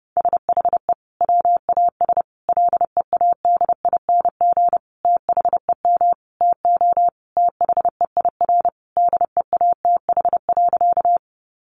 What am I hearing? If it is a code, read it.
Morse code, heard as a keyed tone, about 30 wpm, 700 Hz. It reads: SHE WAS LEADING THEM TO THEIR DEATH.
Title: Sentences from Top 1100 Words 30wpm